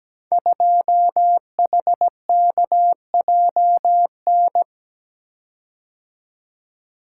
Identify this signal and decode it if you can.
Morse code, heard as a keyed tone, about 17 wpm, 700 Hz. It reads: 2HKJN
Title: Code Groups - 5 Letter:Numbers - 17wpm